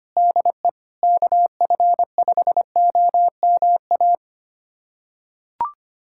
Morse code, heard as a keyed tone, about 25 wpm, 700 Hz. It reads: DE KF5OMA
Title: USA Callsign DE Callsign 25wpm